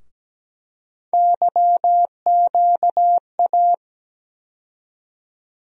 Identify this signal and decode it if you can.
Morse code, heard as a keyed tone, about 17 wpm, 700 Hz. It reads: YQA
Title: USA Call Sign Postfixes - 3 Letters 17wpm